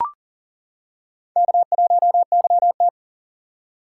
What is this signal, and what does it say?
Morse code, heard as a keyed tone, about 40 wpm, 700 Hz. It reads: K1YT
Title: K1USN SST QSO Practice - 40wpm